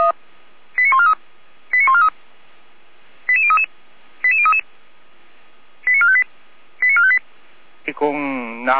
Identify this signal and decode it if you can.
Five-tone selective call (ZVEI), heard as quick runs of five tones, tones of about 70 ms, with speech. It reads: 89143 89143 88033 88033 89458 89458